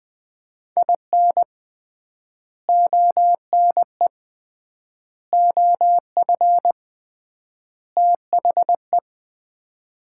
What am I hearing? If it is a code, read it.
Morse code, heard as a keyed tone, about 20 wpm, 700 Hz. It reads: IN ONE OF THE
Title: 3x Word-Spacing - Sets of 4 Words - 20wpm